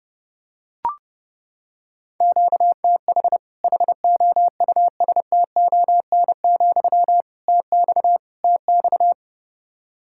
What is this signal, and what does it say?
Morse code, heard as a keyed tone, about 30 wpm, 700 Hz. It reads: QTH HOUSTON, TX TX
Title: Practice QSO - Exchange 1 - Beginning - 30wpm